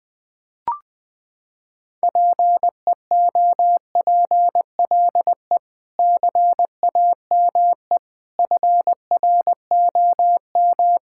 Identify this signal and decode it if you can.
Morse code, heard as a keyed tone, about 19 wpm, 700 Hz. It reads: PEOPLE CAME FROM
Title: Sentences with 5-Word Phrases - 20wpm